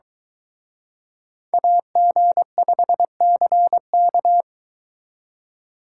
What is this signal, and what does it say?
Morse code, heard as a keyed tone, about 23 wpm, 700 Hz. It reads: AG5CK